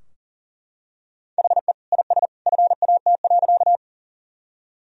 Morse code, heard as a keyed tone, about 40 wpm, 700 Hz. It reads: HE IS FAT.
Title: Sentences from Top 1000 Words - Review Part 4 - 40wpm